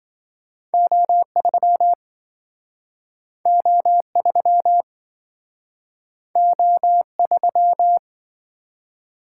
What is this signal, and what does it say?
Morse code, heard as a keyed tone, about 27 wpm, 700 Hz. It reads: O3 O3 O3